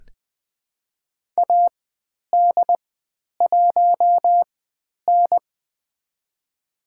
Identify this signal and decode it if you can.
Morse code, heard as a keyed tone, about 20 wpm, spaced as for 10 wpm, 700 Hz. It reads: AD1N
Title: 095 Lesson 16g - Letter c - Call Signs 20wpm